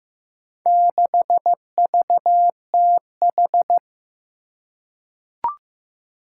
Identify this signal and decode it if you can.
Morse code, heard as a keyed tone, about 15 wpm, 700 Hz. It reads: 6VTH
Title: Four Challenging Letter-Number Combinations 15wpm